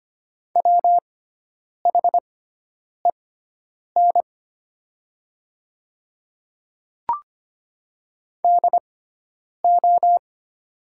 Morse code, heard as a keyed tone, about 25 wpm, spaced as for 9 wpm, 700 Hz. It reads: WHEN DO
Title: 6x Character Spacing - Top 500 Words - 25wpm